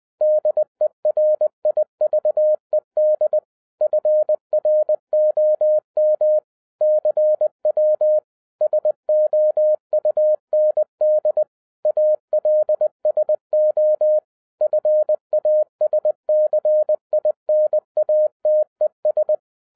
Morse code, heard as a keyed tone, about 20 wpm, 610 Hz. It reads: DERIVED FROM CW SOUND ALSO FASCINATES
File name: 2012-01-mp3 / n05-CAME BACK TO HAM RADIO AFTER 30 YEARS OF INTERVAL-20wpm